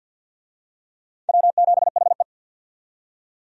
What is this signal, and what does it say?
Morse code, heard as a keyed tone, about 47 wpm, 700 Hz. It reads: W7HE
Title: USA Call Signs - 4 Characters - 50wpm